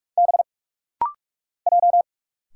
Morse code, heard as a keyed tone, about 46 wpm, 700 Hz. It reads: B J
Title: Single Letters - Rapid-Fire 45wpm